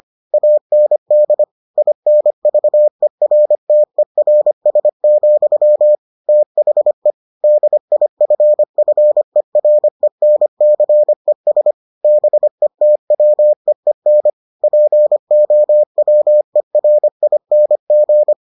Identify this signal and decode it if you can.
Morse code, heard as a keyed tone, about 25 wpm, 600 Hz. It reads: AND INVERTERS, THE DIFFERENCES BETWEEN POWERING